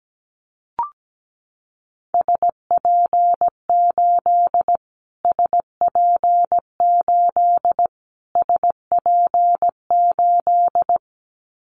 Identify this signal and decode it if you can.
Morse code, heard as a keyed tone, about 17 wpm, 700 Hz. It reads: SP8 SP8 SP8
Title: Callsign Prefixes for Top 25 DXCC - Repeated 3x - 17wpm